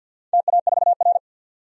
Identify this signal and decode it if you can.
Morse code, heard as a keyed tone, about 51 wpm, 700 Hz. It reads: TA4R
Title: DXCC-100 Call Signs - 4 Characters - 50wpm